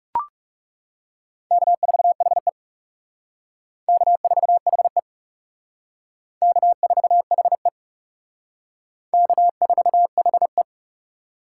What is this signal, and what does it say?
Morse code, heard as a keyed tone, about 38 wpm, 700 Hz. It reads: K4HE K4HE K4HE K4HE